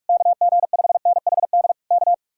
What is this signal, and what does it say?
Morse code, heard as a keyed tone, about 67 wpm, spaced as for 55 wpm, 700 Hz. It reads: KG5NHD K
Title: Speed-Racing Common QSO Phrases Encore - 45wpm